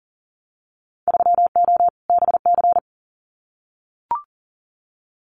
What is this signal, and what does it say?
Morse code, heard as a keyed tone, about 38 wpm, 700 Hz. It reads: 3O BC
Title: Winter Field Day Exchanges Encore 40wpm